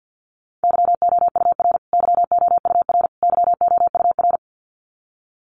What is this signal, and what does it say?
Morse code, heard as a keyed tone, about 50 wpm, 700 Hz. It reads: YOUR YOUR YOUR
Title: Top 500 Words Repeated 50wpm